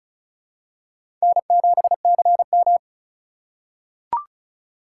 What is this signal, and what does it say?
Morse code, heard as a keyed tone, about 36 wpm, 690 Hz. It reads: N7CM
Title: USA Call Signs - 4 Characters - 3x Character Spacing - 35wpm